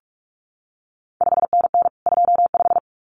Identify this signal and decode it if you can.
Morse code, heard as a keyed tone, about 46 wpm, 700 Hz. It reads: 5NN 25
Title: CQ WW DX Contest Exchange Encore - 45wpm